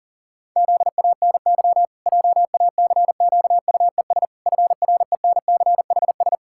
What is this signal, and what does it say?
Morse code, heard as a keyed tone, about 40 wpm, 700 Hz. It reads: ZANY JACQUES FRENCHS
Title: Pangram Challenge v2 - 40wpm